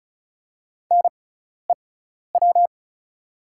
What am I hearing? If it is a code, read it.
Morse code, heard as a keyed tone, about 34 wpm, spaced as for 12 wpm, 700 Hz. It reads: NEW
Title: 6x Character Spacing - Top 200 Words - 35wpm